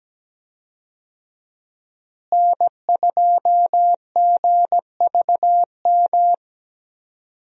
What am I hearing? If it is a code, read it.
Morse code, heard as a keyed tone, about 17 wpm, 700 Hz. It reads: N2GVM
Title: USA Call Signs - 5 Characters - Repeated 2x - 17wpm